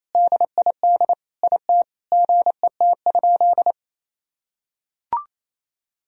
Morse code, heard as a keyed tone, about 27 wpm, 700 Hz. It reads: DID IT GET?
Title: Sentences from Top 1000 Words - Review Part 4 - 28wpm